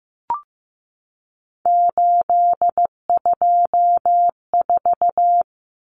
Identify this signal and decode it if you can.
Morse code, heard as a keyed tone, about 15 wpm, 700 Hz. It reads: 824
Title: Common Cut Numbers - 3-Digit 15wpm